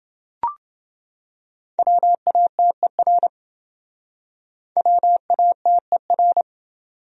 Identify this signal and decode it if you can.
Morse code, heard as a keyed tone, about 28 wpm, 700 Hz. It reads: WATER WATER